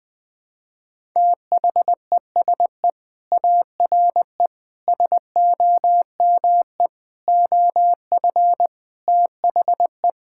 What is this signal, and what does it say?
Morse code, heard as a keyed tone, about 21 wpm, 700 Hz. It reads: THESE ARE SOME OF THE
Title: Sentences with 4-Word Phrases - 20wpm